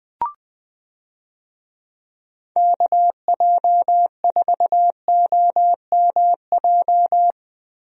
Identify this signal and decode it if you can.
Morse code, heard as a keyed tone, about 21 wpm, 700 Hz. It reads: KJ4OMJ
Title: USA Call Signs - 6 Characters - Repeated 2x - 20wpm